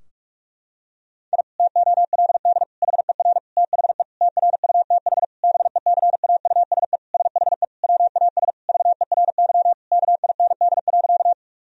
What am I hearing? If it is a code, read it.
Morse code, heard as a keyed tone, about 46 wpm, 700 Hz. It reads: I TOLD HER THE TRUTH BECAUSE SHE WAS VERY KIND.
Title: Sentences from Top 1000 Words - Review Part 3 - 45wpm